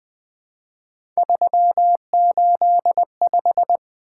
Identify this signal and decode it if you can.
Morse code, heard as a keyed tone, about 20 wpm, 700 Hz. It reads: 385
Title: Three Letter-Number Combinations - 20wpm